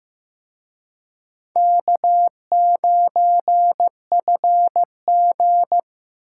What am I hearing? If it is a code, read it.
Morse code, heard as a keyed tone, about 18 wpm, 700 Hz. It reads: K9FG
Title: Speed Racing - USA 1x2 Call Signs 15wpm